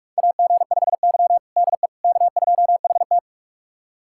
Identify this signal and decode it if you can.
Morse code, heard as a keyed tone, about 46 wpm, 700 Hz. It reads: AG5Y DE K2HT
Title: USA Callsign DE Callsign - 4 Characters - 45wpm